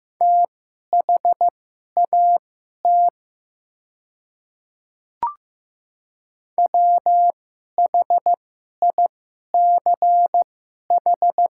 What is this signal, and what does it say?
Morse code, heard as a keyed tone, about 15 wpm, spaced as for 11 wpm, 700 Hz. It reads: THAT WHICH
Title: 2x Character Spacing - Top 100 Words - 15wpm